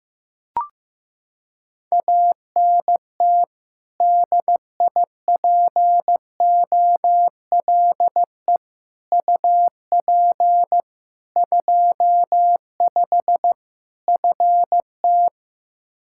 Morse code, heard as a keyed tone, about 15 wpm, 700 Hz. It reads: ANT DIPOLE UP 25 FT
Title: Practice QSO - Exchange 2 - Info - 15wpm